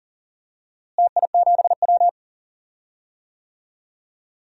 Morse code, heard as a keyed tone, about 39 wpm, 700 Hz. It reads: TI7W
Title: DXCC-100 Call Signs - 4 Characters - Repeated 2x - 40wpm